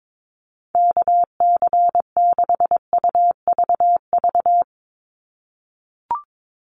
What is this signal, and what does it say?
Morse code, heard as a keyed tone, about 22 wpm, 700 Hz. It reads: KC6UVV
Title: USA Call Signs 22wpm